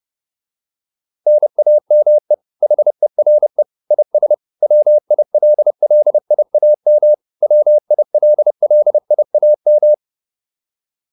Morse code, heard as a keyed tone, about 30 wpm, 600 Hz. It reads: NAME HERE IS WILLIAM WILLIAM
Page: advanced-practice-files